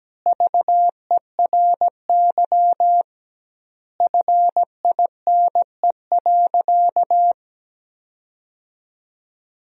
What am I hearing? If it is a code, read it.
Morse code, heard as a keyed tone, about 17 wpm, 700 Hz. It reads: VERY FINE.
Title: 2x Word-Spacing - Sentences from Top 500 Words - 17wpm